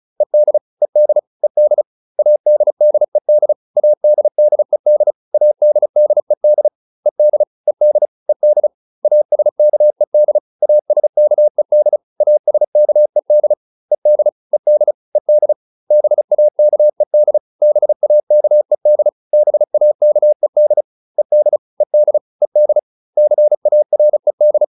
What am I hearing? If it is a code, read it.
Morse code, heard as a keyed tone, about 34 wpm, 600 Hz. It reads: ED ED ED ADDED ADDED ADDED ED ED ED ASKED ASKED ASKED ED ED ED BAKED BAKED BAKED ED ED ED CARED